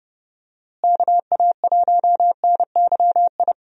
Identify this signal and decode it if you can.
Morse code, heard as a keyed tone, about 32 wpm, 700 Hz. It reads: KA1NYI
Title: USA Call Signs - 6 Characters - Repeated 2x - 30wpm